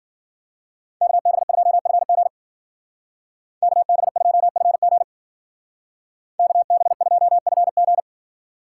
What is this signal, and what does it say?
Morse code, heard as a keyed tone, about 49 wpm, 700 Hz. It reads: KB2FG KB2FG KB2FG
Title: Speed Racing - USA 2x2 Call Signs 40wpm